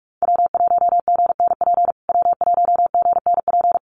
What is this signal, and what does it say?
Morse code, heard as a keyed tone, about 44 wpm, 690 Hz. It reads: W1GNP W1GNP
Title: USA Call Signs Repeated 3x 45wpm